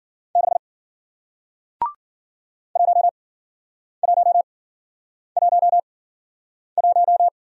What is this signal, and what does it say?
Morse code, heard as a keyed tone, about 53 wpm, spaced as for 9 wpm, 700 Hz. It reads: 6 1111